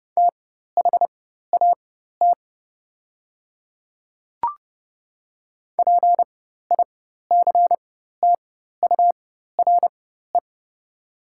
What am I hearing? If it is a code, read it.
Morse code, heard as a keyed tone, about 30 wpm, spaced as for 14 wpm, 700 Hz. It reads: THAT PICTURE
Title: 4x Character Spacing - Top 200 Words - 30wpm